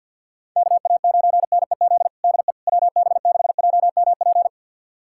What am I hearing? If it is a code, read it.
Morse code, heard as a keyed tone, about 47 wpm, 700 Hz. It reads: KA9NEZ DE WB6JMP
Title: USA Callsign DE Callsign 50wpm